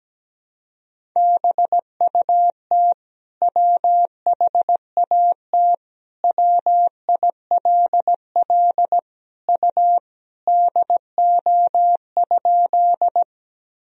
Morse code, heard as a keyed tone, about 17 wpm, 700 Hz. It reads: BUT WHAT WILL U DO?
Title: Sentences from Top 100 Words and Abbreviations 17wpm